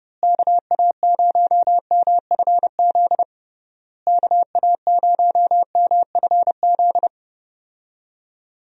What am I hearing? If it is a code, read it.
Morse code, heard as a keyed tone, about 30 wpm, 700 Hz. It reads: KA0MFZ KA0MFZ
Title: USA Call Signs - 6 Characters - Repeated 3x - 30wpm